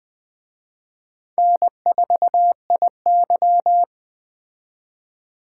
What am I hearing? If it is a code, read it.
Morse code, heard as a keyed tone, about 20 wpm, 700 Hz. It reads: N4IY